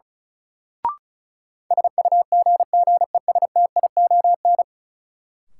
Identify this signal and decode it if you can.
Morse code, heard as a keyed tone, about 36 wpm, 700 Hz. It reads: SUGGESTION
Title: Sentences from Top 1600 Words - New Word Drill Alt - 35wpm